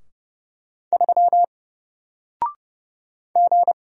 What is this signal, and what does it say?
Morse code, heard as a keyed tone, about 30 wpm, 700 Hz. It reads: 3 G
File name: Single-Letter-Number-30wpm